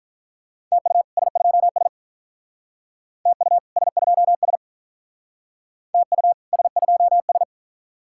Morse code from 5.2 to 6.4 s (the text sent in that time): TU